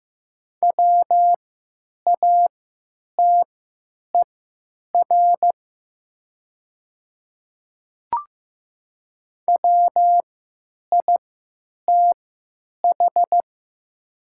Morse code, heard as a keyed tone, about 15 wpm, spaced as for 9 wpm, 700 Hz. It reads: WATER WITH
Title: 3x Character Spacing - Top 100 Words - 15wpm